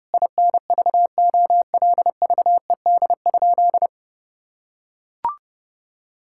Morse code, heard as a keyed tone, about 30 wpm, 700 Hz. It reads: INVOLVED?
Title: Sentences from Top 1000 Words - Review Part 5 - 30wpm